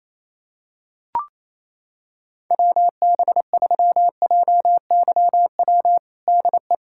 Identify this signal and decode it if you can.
Morse code, heard as a keyed tone, about 28 wpm, 700 Hz. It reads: WB3JYW DE
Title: USA Callsign DE Callsign - 6 Characters - 28wpm